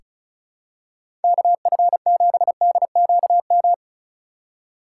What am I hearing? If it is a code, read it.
Morse code, heard as a keyed tone, about 36 wpm, 700 Hz. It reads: KF7DQM
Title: USA Call Signs - 2x Character Spacing - 35wpm